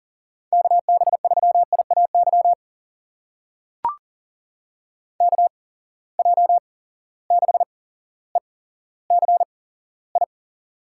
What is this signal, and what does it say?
Morse code, heard as a keyed tone, about 38 wpm, spaced as for 17 wpm, 700 Hz. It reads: KB3IAY KJ6ECI
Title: USA Call Signs - 6 Characters - 8x Character Spacing - 40wpm